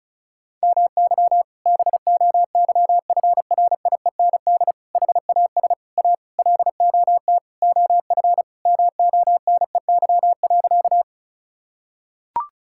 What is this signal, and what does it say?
Morse code, heard as a keyed tone, about 36 wpm, 700 Hz. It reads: MY BOYFRIEND HAS A LOT OF MONEY.
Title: Sentences from Top 1000 Words - Review Part 5 - 35wpm